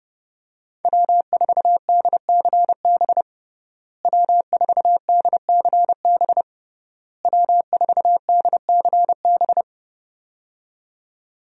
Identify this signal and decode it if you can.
Morse code, heard as a keyed tone, about 30 wpm, 700 Hz. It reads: W4DCB W4DCB W4DCB
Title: USA Call Signs - 5 Characters - Repeated 3x - 30wpm